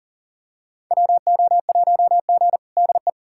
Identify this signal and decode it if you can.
Morse code, heard as a keyed tone, about 40 wpm, 700 Hz. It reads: WO1G DE
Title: USA Callsign DE Callsign - 4 Characters - 40wpm